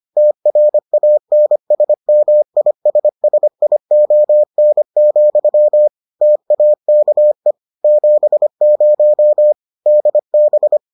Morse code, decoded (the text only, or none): TRANSMISSION, TAKE 70 DB